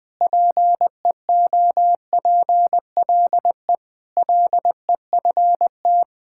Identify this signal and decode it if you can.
Morse code, heard as a keyed tone, about 20 wpm, 700 Hz. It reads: PEOPLE LEFT
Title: Sentences with 3-Word Phrases - Spoken Beginning - 20wpm